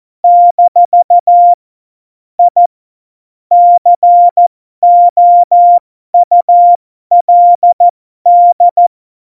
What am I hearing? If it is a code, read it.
Morse code, heard as a keyed tone, about 14 wpm, spaced as for 12 wpm, 700 Hz. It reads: - I COULD
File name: Morsekurs / Witze